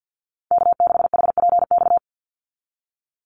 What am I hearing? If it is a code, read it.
Morse code, heard as a keyed tone, about 50 wpm, 700 Hz. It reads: K6HPX
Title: NA SOTA Activator Callsigns 50wpm